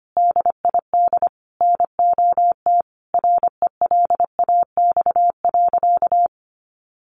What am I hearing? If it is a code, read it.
Morse code, heard as a keyed tone, about 25 wpm, 700 Hz. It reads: DID NOT RELAX.
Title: A Sound Of Thunder 25wpm